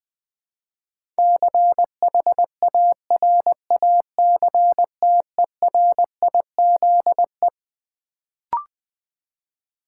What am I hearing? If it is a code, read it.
Morse code, heard as a keyed tone, about 20 wpm, 700 Hz. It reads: CHARACTERIZE
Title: Build A Word - 4 Letters At A Time - 12-Letter Words - 20wpm